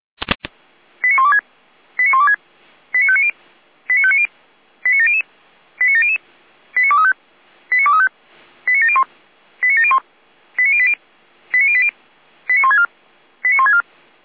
Five-tone selective call (ZVEI), heard as quick runs of five tones, tones of about 80 ms. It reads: 89126 89126 89590 89590 89700 89700 89235 89235 89701 89701 89080 89080 89164 89164
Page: SDR/ZVEI